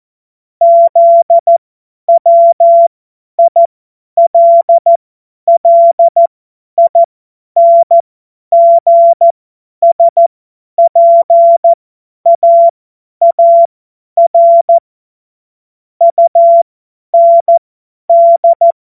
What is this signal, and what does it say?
Morse code, heard as a keyed tone, about 14 wpm, spaced as for 10 wpm, 680 Hz. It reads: ZWILLINGSPAAR UND
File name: Morsekurs / Witze / Witz 05 März 2024 50 F70